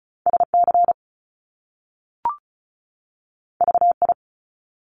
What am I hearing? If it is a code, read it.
Morse code, heard as a keyed tone, about 36 wpm, 700 Hz. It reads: SC VI
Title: Top Two Letter Combinations in Words 35wpm